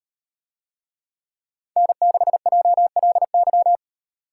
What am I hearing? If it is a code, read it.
Morse code, heard as a keyed tone, about 36 wpm, 700 Hz. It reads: N6JLY